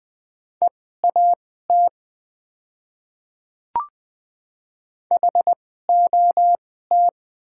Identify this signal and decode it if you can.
Morse code, heard as a keyed tone, about 20 wpm, spaced as for 14 wpm, 700 Hz. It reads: EAT HOT
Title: 2x Character Spacing - Top 500 Words - 20wpm